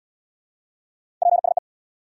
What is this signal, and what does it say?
Morse code, heard as a keyed tone, about 69 wpm, 700 Hz. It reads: PSE